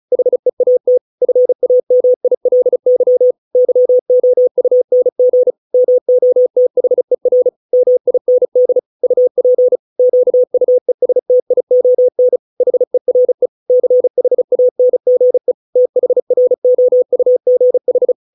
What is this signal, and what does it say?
Morse code, heard as a keyed tone, about 34 wpm, 500 Hz. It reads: HEAT FAMILY YOUNG MOTHER MIND UP QUESTION HERE CHANGE THROUGH